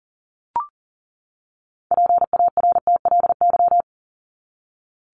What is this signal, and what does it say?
Morse code, heard as a keyed tone, about 40 wpm, 700 Hz. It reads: PARTLY
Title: LY Suffix - Top 100 Words with Base Words 40wpm